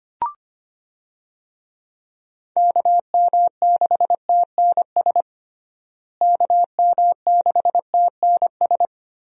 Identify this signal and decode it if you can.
Morse code, heard as a keyed tone, about 25 wpm, 700 Hz. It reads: KM6TNS KM6TNS
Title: USA Call Signs - 6 Characters - Repeated 2x - 25wpm